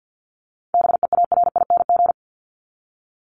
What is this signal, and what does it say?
Morse code, heard as a keyed tone, about 49 wpm, 700 Hz. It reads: BEARING
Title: ING Suffix - Top 100 Words with Base Words 50wpm